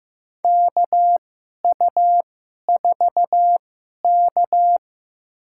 Morse code, heard as a keyed tone, about 15 wpm, spaced as for 11 wpm, 700 Hz. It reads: KU4K
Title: USA Call Signs - 2x Character Spacing - 15wpm